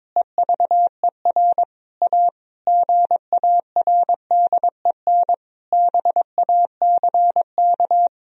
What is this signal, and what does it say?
Morse code, heard as a keyed tone, about 22 wpm, 700 Hz. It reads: EVER A GARDEN BACK
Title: Sentences from Top 1000 Words - Review Part 4 - 22wpm